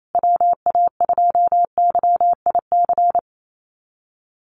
Morse code, heard as a keyed tone, about 28 wpm, 700 Hz. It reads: WA2YIC
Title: USA Call Signs Repeated 3x 28wpm